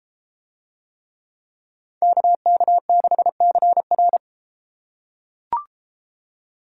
Morse code, heard as a keyed tone, about 33 wpm, 700 Hz. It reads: KK6CR